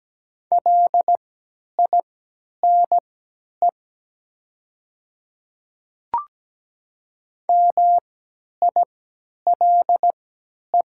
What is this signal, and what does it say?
Morse code, heard as a keyed tone, about 17 wpm, spaced as for 10 wpm, 700 Hz. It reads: LINE MILE
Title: 3x Character Spacing - Top 300 Words - 17wpm